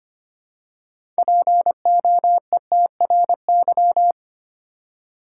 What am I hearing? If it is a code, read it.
Morse code, heard as a keyed tone, about 25 wpm, 700 Hz. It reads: POETRY
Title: Six Letter Words - 25wpm